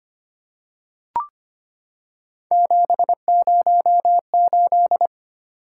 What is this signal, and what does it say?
Morse code, heard as a keyed tone, about 25 wpm, 700 Hz. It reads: 708
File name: Three Letter-Number Combinations - 25wpm